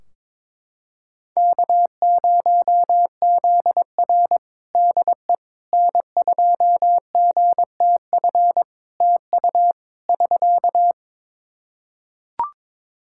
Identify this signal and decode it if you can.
Morse code, heard as a keyed tone, about 22 wpm, 700 Hz. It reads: K0ZR DE N2GTF TU <SK>
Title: Practice QSO - Exchange 4 - Ending - 22wpm